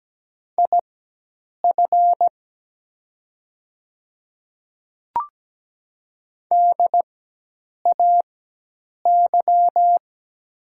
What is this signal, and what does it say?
Morse code, heard as a keyed tone, about 17 wpm, spaced as for 8 wpm, 700 Hz. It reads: IF DAY
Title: 4x Character Spacing - Top 100 Words - 17wpm